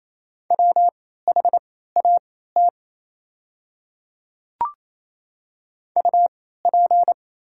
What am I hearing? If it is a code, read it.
Morse code, heard as a keyed tone, about 27 wpm, spaced as for 16 wpm, 690 Hz. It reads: WHAT UP
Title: 3x Character Spacing - Top 100 Words - 28wpm